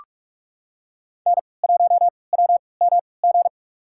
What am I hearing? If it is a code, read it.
Morse code, heard as a keyed tone, about 46 wpm, spaced as for 25 wpm, 700 Hz. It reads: N1WMG
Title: USA Call Signs - 3x Character Spacing - 45wpm